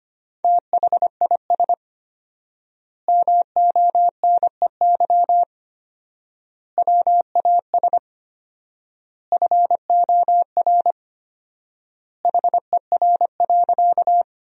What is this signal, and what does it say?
Morse code, heard as a keyed tone, about 25 wpm, 700 Hz. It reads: THIS MONEY WAS FOR HER.
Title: 4x Word-Spacing - Sentences from Top 400 Words - 25wpm